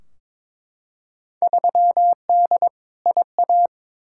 Morse code, heard as a keyed tone, about 22 wpm, 700 Hz. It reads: 3D IA
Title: Field Day Exchanges Encore - 22wpm